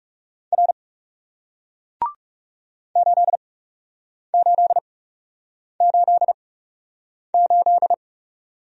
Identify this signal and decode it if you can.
Morse code, heard as a keyed tone, about 40 wpm, spaced as for 8 wpm, 700 Hz. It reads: R 8888